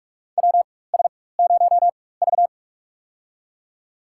Morse code, heard as a keyed tone, about 46 wpm, spaced as for 21 wpm, 700 Hz. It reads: WS0V
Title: USA Call Signs - 4 Characters - 4x Character Spacing - 45wpm